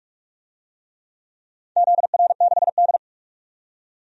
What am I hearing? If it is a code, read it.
Morse code, heard as a keyed tone, about 42 wpm, 700 Hz. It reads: ZR6D
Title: DXCC-100 Call Signs - 4 Characters - 45wpm